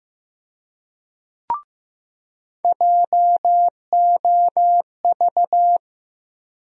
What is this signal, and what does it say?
Morse code, heard as a keyed tone, about 16 wpm, 700 Hz. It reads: JOV
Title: USA Call Sign Postfixes - 3 Letters 15wpm